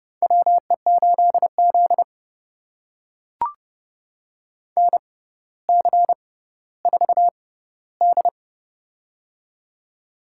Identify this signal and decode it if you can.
Morse code, heard as a keyed tone, about 30 wpm, spaced as for 15 wpm, 700 Hz. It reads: WE8Z NC4D
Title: USA Call Signs - 6x Character Spacing - 30wpm